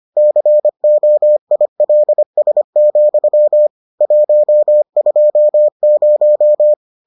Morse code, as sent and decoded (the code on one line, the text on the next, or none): -.-. --- .. .-.. ... --..-- / .---- ..--- -----
COILS, 120